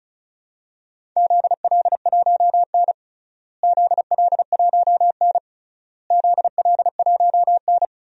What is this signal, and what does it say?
Morse code, heard as a keyed tone, about 36 wpm, 700 Hz. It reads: ZL1N ZL1N ZL1N
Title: DXCC-100 Call Signs - 4 Characters - Repeated 3x - 35wpm